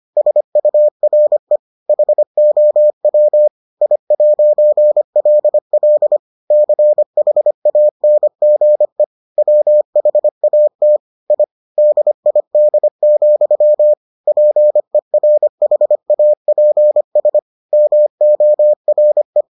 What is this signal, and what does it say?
Morse code, heard as a keyed tone, about 25 wpm, 600 Hz. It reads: SURE HOW I'LL CHANGE WHAT I DID, PERHAPS MORE